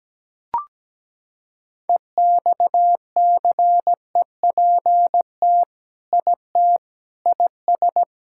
Morse code, heard as a keyed tone, about 17 wpm, 700 Hz. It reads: EXCEPT IT IS
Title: Sentences from Top 1600 Words 17wpm